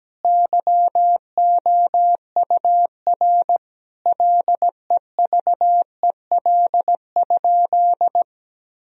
Morse code, heard as a keyed tone, about 17 wpm, 700 Hz. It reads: YOUR LEVEL?
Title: Sentences from Top 1000 Words - Review Part 4 - 17wpm